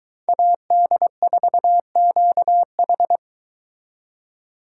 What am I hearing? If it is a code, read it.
Morse code, heard as a keyed tone, about 22 wpm, 700 Hz. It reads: AD4QH